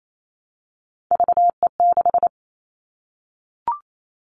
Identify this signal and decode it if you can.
Morse code, heard as a keyed tone, about 28 wpm, 700 Hz. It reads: VE6
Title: Callsign Prefixes for Top 25 DXCC - Repeated 3x - 28wpm